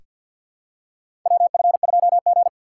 Morse code, heard as a keyed tone, about 47 wpm, 700 Hz. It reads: WF2G
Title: USA Call Signs - 4 Characters - 2x Character Spacing - 50wpm